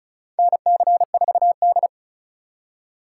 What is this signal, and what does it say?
Morse code, heard as a keyed tone, about 36 wpm, 700 Hz. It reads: NC4D
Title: USA Call Signs - 4 Characters - 3x Character Spacing - 35wpm